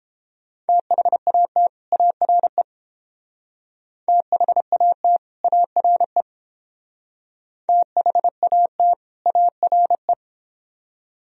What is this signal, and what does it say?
Morse code, heard as a keyed tone, about 30 wpm, 700 Hz. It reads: THAT ARE THAT ARE THAT ARE